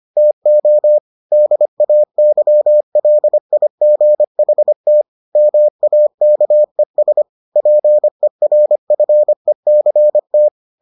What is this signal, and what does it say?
Morse code, heard as a keyed tone, about 25 wpm, 600 Hz. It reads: TO DAYLIGHT MAKES PERFECT